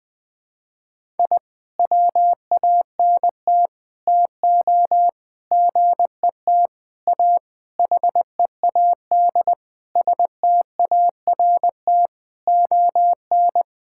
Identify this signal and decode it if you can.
Morse code, heard as a keyed tone, about 20 wpm, 700 Hz. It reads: I WANT TO GET A HEAD START ON
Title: Sentences with 9-Word Phrases - Spoken Beginning - 20wpm